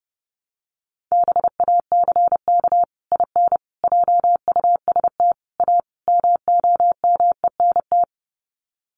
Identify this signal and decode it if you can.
Morse code, heard as a keyed tone, about 30 wpm, 700 Hz. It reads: BACK IN JUST A MOMENT
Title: Sets of 5 Words Encore 30wpm